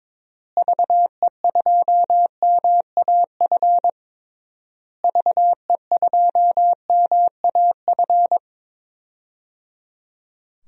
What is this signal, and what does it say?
Morse code, heard as a keyed tone, about 22 wpm, 700 Hz. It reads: VE2MAF VE2MAF
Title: DXCC-100 Call Signs - 6 Characters - Repeated 3x - 22wpm